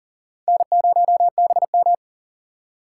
Morse code, heard as a keyed tone, about 40 wpm, 700 Hz. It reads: N0BM
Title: USA Call Signs - 4 Characters - 6x Character Spacing - 40wpm